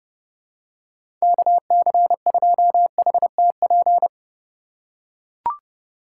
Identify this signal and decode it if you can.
Morse code, heard as a keyed tone, about 30 wpm, 700 Hz. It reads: KC2HTP
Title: USA Call Signs - 6 Characters - 2x Character Spacing - 30wpm